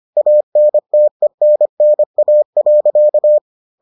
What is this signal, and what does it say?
Morse code, heard as a keyed tone, about 25 wpm, 600 Hz. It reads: ANTENNA.